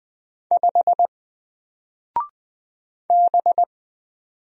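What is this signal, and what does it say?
Morse code, heard as a keyed tone, about 20 wpm, 700 Hz. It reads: 5 B
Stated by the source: Single letter and number Morse drill at 20 wpm, from N4LCD.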